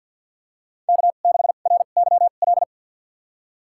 Morse code, heard as a keyed tone, about 49 wpm, spaced as for 35 wpm, 700 Hz. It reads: K6RYL
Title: USA Call Signs - 2x Character Spacing - 50wpm